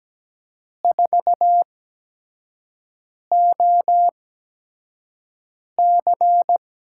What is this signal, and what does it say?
Morse code, heard as a keyed tone, about 17 wpm, spaced as for 5 wpm, 700 Hz. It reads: 4OC
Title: USA Call Signs - 5 Characters - 8x Character Spacing - 17wpm